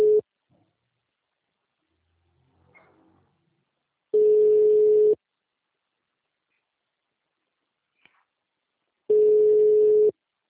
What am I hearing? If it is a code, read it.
Ringback tone (Europe, 425 Hz).